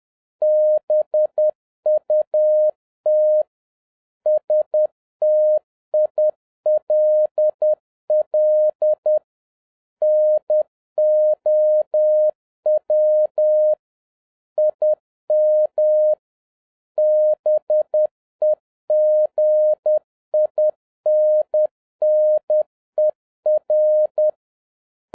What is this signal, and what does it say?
Morse code, heard as a keyed tone, about 10 wpm, 610 Hz. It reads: BUT STILL NOW IM BEGINNER